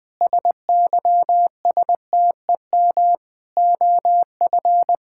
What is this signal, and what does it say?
Morse code, heard as a keyed tone, about 21 wpm, 700 Hz. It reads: SYSTEM OF
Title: Sentences from Top 1000 Words - Review Part 6 - 20wpm